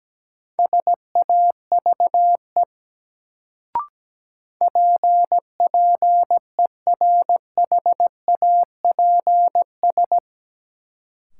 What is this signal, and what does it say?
Morse code, heard as a keyed tone, about 17 wpm, 700 Hz. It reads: SAVE PPERHAPS
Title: Sentences from Top 1100 Words - New Word Drill Alt - 17wpm